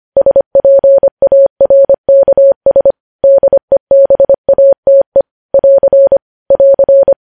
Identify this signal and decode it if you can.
Morse code, heard as a keyed tone, about 25 wpm, 550 Hz. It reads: SPARKS DEBATE <AR> <AR>
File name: podcast / News headlines-1x-25x25